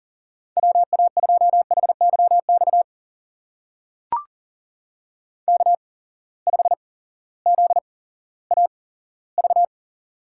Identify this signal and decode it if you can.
Morse code, heard as a keyed tone, about 40 wpm, spaced as for 18 wpm, 700 Hz. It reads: WA2HYX K5ZAV
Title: USA Call Signs - 8x Character Spacing - 40wpm